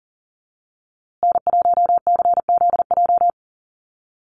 Morse code, heard as a keyed tone, about 40 wpm, 700 Hz. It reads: N1CZJ
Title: USA Call Signs Repeated 3x 40wpm